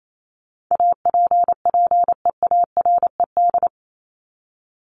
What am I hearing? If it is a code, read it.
Morse code, heard as a keyed tone, about 28 wpm, 700 Hz. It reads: APPEARED
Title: ED Suffix - Top 100 Words with Base Words 28wpm